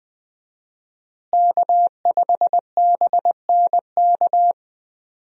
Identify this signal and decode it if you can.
Morse code, heard as a keyed tone, about 20 wpm, 700 Hz. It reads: K5BNK
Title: USA Call Signs - 5 Characters - Repeated 3x - 20wpm